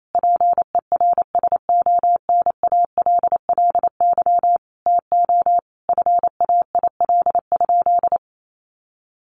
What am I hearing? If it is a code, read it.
Morse code, heard as a keyed tone, about 28 wpm, 700 Hz. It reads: PERSONALLY TO FAIL?
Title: Sentences from Top 1400 Words 28wpm